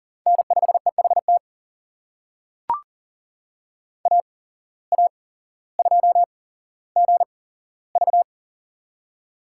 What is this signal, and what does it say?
Morse code, heard as a keyed tone, about 38 wpm, spaced as for 17 wpm, 700 Hz. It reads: N5EHT AA2GV
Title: USA Call Signs - 5 Characters - 8x Character Spacing - 40wpm